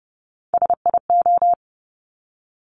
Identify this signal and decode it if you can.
Morse code, heard as a keyed tone, about 30 wpm, 690 Hz. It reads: SIO
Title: Top Three Letter Combinations in Words 30wpm